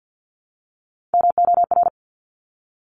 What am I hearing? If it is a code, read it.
Morse code, heard as a keyed tone, about 50 wpm, 700 Hz. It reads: MOR
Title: Top Three Letter Combinations in Words 50wpm